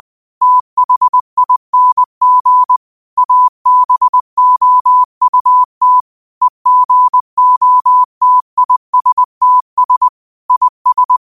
Morse code, heard as a keyed tone, about 20 wpm, 1000 Hz. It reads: THING ABOUT EGOTISTS IS
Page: New quotes every day in morse code at 20 Words per minute.